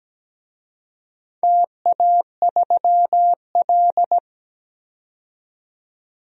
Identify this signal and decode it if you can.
Morse code, heard as a keyed tone, about 17 wpm, 700 Hz. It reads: TA3L
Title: DXCC-100 Call Signs - 4 Characters - 17wpm